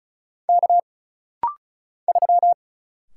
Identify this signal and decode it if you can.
Morse code, heard as a keyed tone, about 36 wpm, 700 Hz. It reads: K 3
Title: Single Letter-Number - Rapid-Fire 35wpm